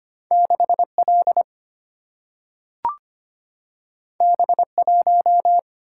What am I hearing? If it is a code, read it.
Morse code, heard as a keyed tone, about 25 wpm, 700 Hz. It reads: 6L B1
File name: Two Letter-Number Combinations - 25wpm